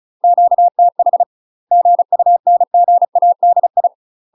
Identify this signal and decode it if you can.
Morse code, heard as a keyed tone, about 34 wpm, 700 Hz. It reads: QTH GUNGADI
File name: UK MASTER QSOs TRAINING / 35WPM / UK MASTER QSOs 0001